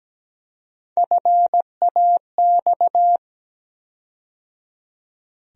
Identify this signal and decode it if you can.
Morse code, heard as a keyed tone, about 17 wpm, 700 Hz. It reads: FAX
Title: USA Call Sign Postfixes - 3 Letters 17wpm